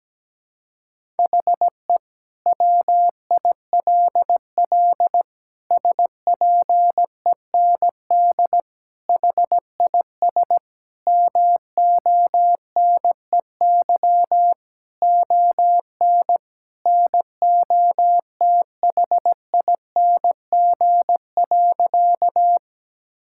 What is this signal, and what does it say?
Morse code, heard as a keyed tone, about 17 wpm, 700 Hz. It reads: HE WILL SPEND HIS MONEY ON NOTHING.
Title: Sentences from Top 1000 Words - Review Part 5 - 17wpm